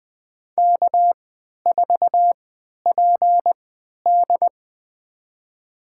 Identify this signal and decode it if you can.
Morse code, heard as a keyed tone, about 20 wpm, spaced as for 11 wpm, 700 Hz. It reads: K4PD
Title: USA Call Signs - 4 Characters - 3x Character Spacing - 20wpm